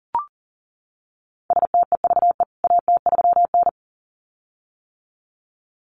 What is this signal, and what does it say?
Morse code, heard as a keyed tone, about 40 wpm, 700 Hz. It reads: STEVE AT3N
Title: CWT Exchange 40wpm